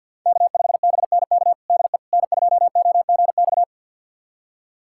Morse code, heard as a keyed tone, about 53 wpm, 700 Hz. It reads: K5BNK DE N2OGX
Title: USA Callsign DE Callsign - 5 Characters - 50wpm